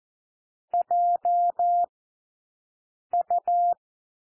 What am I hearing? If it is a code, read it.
Morse code, heard as a keyed tone, about 14 wpm, 700 Hz. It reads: J U
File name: cwmp3 / W1 SOUND WORDS 14 wpm 0